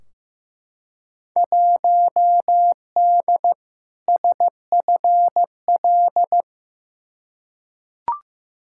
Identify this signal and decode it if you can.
Morse code, heard as a keyed tone, about 15 wpm, 700 Hz. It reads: 1D SFL
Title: Field Day Exchanges - 15wpm